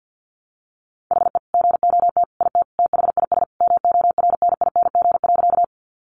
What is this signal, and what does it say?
Morse code, heard as a keyed tone, about 50 wpm, 700 Hz. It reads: HE GOT IT THIS MORNING.
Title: Sentences from Top 400 Words 50wpm